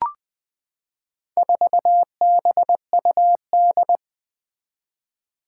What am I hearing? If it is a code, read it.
Morse code, heard as a keyed tone, about 20 wpm, 700 Hz. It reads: 4BUD
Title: Four Challenging Letter-Number Combinations 20wpm